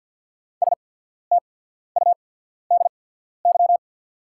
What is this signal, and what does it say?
Morse code, heard as a keyed tone, about 48 wpm, spaced as for 14 wpm, 690 Hz. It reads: STUDY
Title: 8x Character Spacing - Top 300 Words - 50wpm